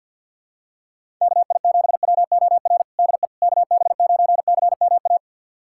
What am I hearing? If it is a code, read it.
Morse code, heard as a keyed tone, about 51 wpm, 700 Hz. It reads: KI7WOR DE KB9CMA
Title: USA Callsign DE Callsign - 6 Characters - 50wpm